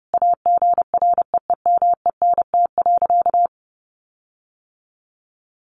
Morse code, heard as a keyed tone, about 30 wpm, 700 Hz. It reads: AGREEMENT.
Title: Sentences from Top 1500 Words 30wpm